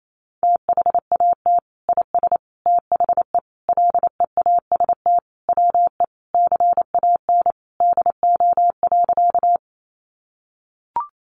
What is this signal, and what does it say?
Morse code, heard as a keyed tone, about 28 wpm, 700 Hz. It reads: THAT IS THE LEAST WE CAN DO.
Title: Sentences from Top 700 Words 28wpm